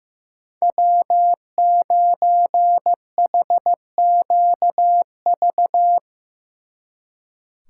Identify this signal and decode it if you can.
Morse code, heard as a keyed tone, about 16 wpm, 700 Hz. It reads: W9HQV
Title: Speed Racing - USA Call Signs 15wpm